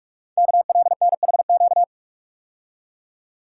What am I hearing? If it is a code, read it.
Morse code, heard as a keyed tone, about 44 wpm, 700 Hz. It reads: KLNHQ
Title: Code Groups - 5 Letters - 45wpm